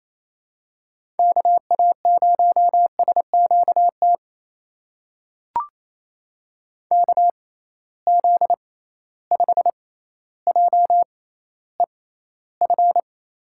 Morse code, heard as a keyed tone, about 28 wpm, spaced as for 14 wpm, 700 Hz. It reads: KA0SQT KZ5JEF
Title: USA Call Signs - 6 Characters - 6x Character Spacing - 28wpm